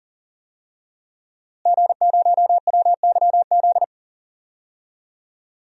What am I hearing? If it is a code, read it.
Morse code, heard as a keyed tone, about 40 wpm, 700 Hz. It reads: G0WYZ
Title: DXCC-100 Call Signs - 5 Characters - Repeated 3x - 40wpm